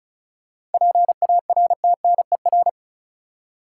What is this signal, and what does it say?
Morse code, heard as a keyed tone, about 36 wpm, 700 Hz. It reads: PARTNER
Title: Seven Letter Words - 35wpm